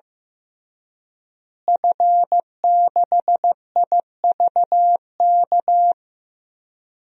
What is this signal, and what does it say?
Morse code, heard as a keyed tone, about 14 wpm, 700 Hz. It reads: F6IVK
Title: DXCC-100 Call Signs - 5 Characters - Repeated 3x - 15wpm